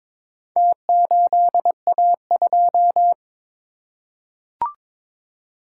Morse code, heard as a keyed tone, about 21 wpm, 700 Hz. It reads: T8A2
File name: practice / 22wpm / Four Letter-Number Combinations - 22wpm